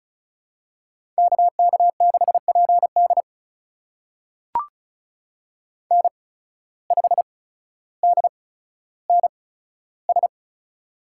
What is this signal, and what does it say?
Morse code, heard as a keyed tone, about 36 wpm, spaced as for 15 wpm, 700 Hz. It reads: KK6PD N5DNS
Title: USA Call Signs - 5 Characters - 8x Character Spacing - 35wpm